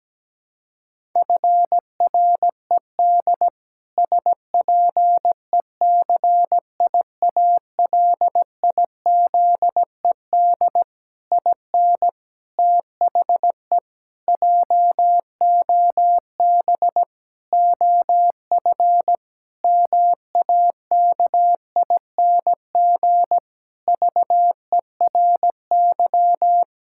Morse code, heard as a keyed tone, about 17 wpm, 700 Hz. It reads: FRED SPECIALIZED IN THE JOB OF MAKING VERY
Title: Pangram Challenge v2 - 17wpm